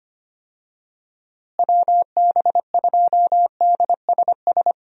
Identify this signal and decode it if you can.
Morse code, heard as a keyed tone, about 25 wpm, 700 Hz. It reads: WB2DSS
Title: USA Call Signs - 6 Characters - Repeated 2x - 25wpm